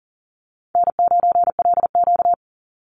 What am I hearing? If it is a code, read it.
Morse code, heard as a keyed tone, about 40 wpm, 700 Hz. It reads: N9LQ
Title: Call Signs Repeated 2x - 40wpm